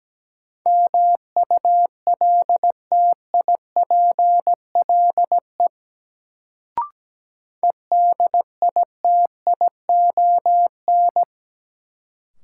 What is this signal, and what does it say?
Morse code, heard as a keyed tone, about 17 wpm, 700 Hz. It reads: MULTIPLE EDITION
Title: Sets of 5 Words - New Word Drill - 17wpm